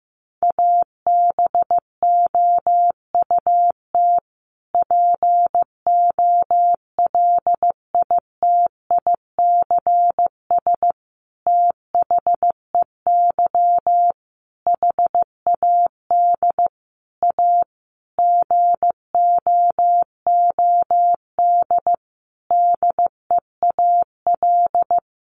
Morse code, heard as a keyed tone, about 15 wpm, 700 Hz. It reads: ABOUT POLITICS THEY HAD A GOOD DEAL
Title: Sentences from Top 1000 Words 15wpm